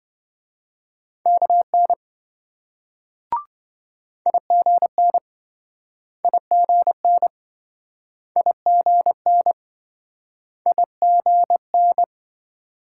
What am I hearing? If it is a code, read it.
Morse code, heard as a keyed tone, about 27 wpm, 700 Hz. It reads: KN IGN IGN IGN IGN